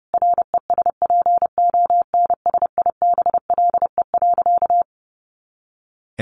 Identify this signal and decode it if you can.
Morse code, heard as a keyed tone, about 28 wpm, 700 Hz. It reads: RESPONSIBLE.
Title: A Sound Of Thunder 30wpm